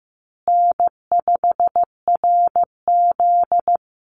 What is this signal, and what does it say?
Morse code, heard as a keyed tone, about 15 wpm, 700 Hz. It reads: N5RZ
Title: Easy Call Signs Repeated 2x - 15wpm